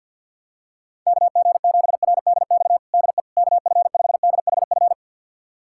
Morse code, heard as a keyed tone, about 52 wpm, 700 Hz. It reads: KG7RDX DE KF5DHF
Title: USA Callsign DE Callsign - 6 Characters - 50wpm